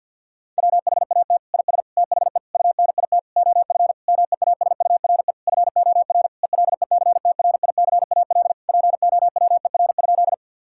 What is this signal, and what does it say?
Morse code, heard as a keyed tone, about 50 wpm, 700 Hz. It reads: WHAT IS THE UNIT OF MEASURE FOR ELECTRICAL POWER?
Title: Sentences from Top 1000 Words - Review Part 6 - 50wpm